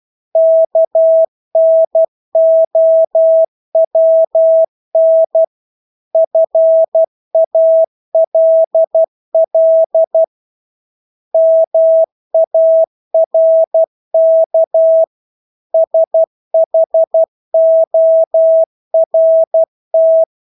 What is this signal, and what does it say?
Morse code, heard as a keyed tone, about 12 wpm, 650 Hz. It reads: KNOWN FALL MARK SHORT
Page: Engelske ord 12 wpm